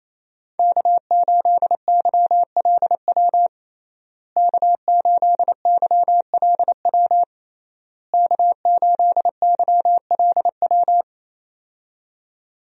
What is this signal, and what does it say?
Morse code, heard as a keyed tone, about 28 wpm, 700 Hz. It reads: K8YLW K8YLW K8YLW
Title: USA Call Signs - 5 Characters - Repeated 3x - 28wpm